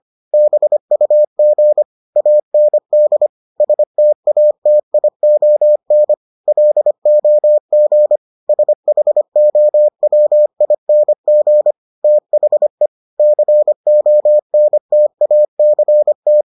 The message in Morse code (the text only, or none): BUG AND STATION LOG SHOWING THE CONTACT